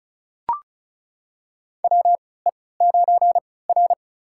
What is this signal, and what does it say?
Morse code, heard as a keyed tone, about 36 wpm, spaced as for 20 wpm, 700 Hz. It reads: WE9R
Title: USA Call Signs - 4 Characters - 3x Character Spacing - 35wpm